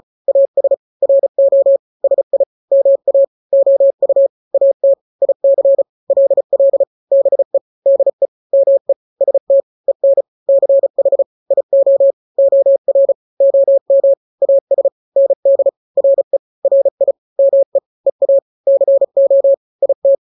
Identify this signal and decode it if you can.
Morse code, heard as a keyed tone, about 34 wpm, 550 Hz. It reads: AS RO SI MA OU AT IC LL BE DE ME ST EN CH IO OR OM AS ND RE RI ME EA CO IT
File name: n-grams / 2-grams 35 30 WPM